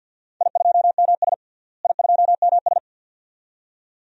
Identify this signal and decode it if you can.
Morse code, heard as a keyed tone, about 46 wpm, 700 Hz. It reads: I2MS I2MS
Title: DXCC-100 Call Signs - 4 Characters - Repeated 2x - 50wpm